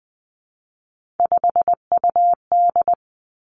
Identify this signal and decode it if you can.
Morse code, heard as a keyed tone, about 20 wpm, 700 Hz. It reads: 5UD
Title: Three Challenging Letter-Number Combinations 20wpm